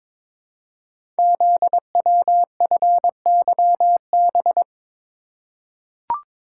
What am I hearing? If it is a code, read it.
Morse code, heard as a keyed tone, about 22 wpm, 700 Hz. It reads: ZWFYB
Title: Code Groups - 5 Letters - 22wpm